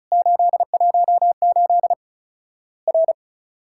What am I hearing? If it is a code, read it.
Morse code, heard as a keyed tone, about 34 wpm, 700 Hz. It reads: 818 R
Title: Worked All Europe DX Contest - QTC Exchange - 35wpm